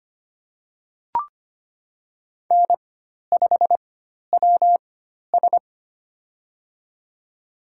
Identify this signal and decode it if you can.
Morse code, heard as a keyed tone, about 25 wpm, spaced as for 12 wpm, 700 Hz. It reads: N5WS
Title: USA Call Signs - 4x Character Spacing - 25wpm